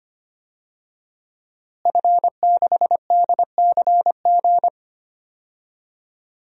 Morse code, heard as a keyed tone, about 25 wpm, 700 Hz. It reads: F6DCG
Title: DXCC-100 Call Signs - 5 Characters - Repeated 2x - 25wpm